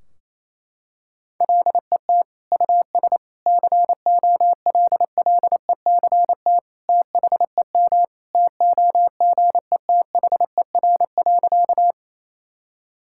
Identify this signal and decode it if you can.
Morse code, heard as a keyed tone, about 28 wpm, 700 Hz. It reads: LET US COLLECT THEM TOGETHER.
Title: Sentences from Top 1600 Words 28wpm